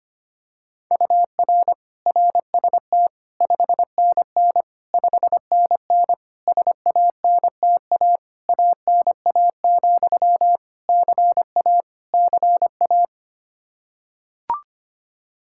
Morse code, heard as a keyed tone, about 25 wpm, 700 Hz. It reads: UR RST 5NN 5NN SANTA ANA, CA CA
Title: Practice QSO - Exchange 1 - Beginning - 25wpm